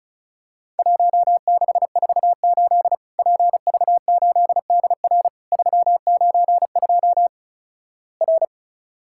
Morse code, heard as a keyed tone, about 36 wpm, 700 Hz. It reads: 1648 PV8DR 392 R
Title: Worked All Europe DX Contest - QTC Exchange - 35wpm